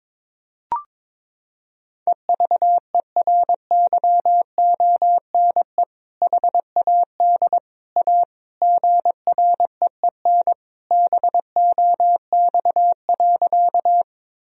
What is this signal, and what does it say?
Morse code, heard as a keyed tone, about 22 wpm, 700 Hz. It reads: EVERYONE HAD A GREEN BOX.
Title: Sentences from Top 1000 Words - Review Part 2 - 22wpm